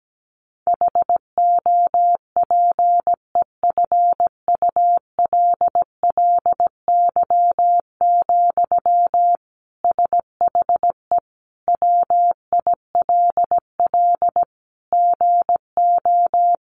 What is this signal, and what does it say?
Morse code, heard as a keyed tone, about 17 wpm, 700 Hz. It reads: HOPEFULLY, SHE WILL GO
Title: Sentences from Top 1500 Words 17wpm